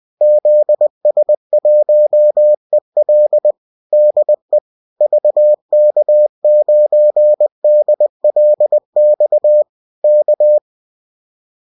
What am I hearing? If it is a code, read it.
Morse code, heard as a keyed tone, about 20 wpm, 600 Hz. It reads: ZS1EL DE VK9DLX K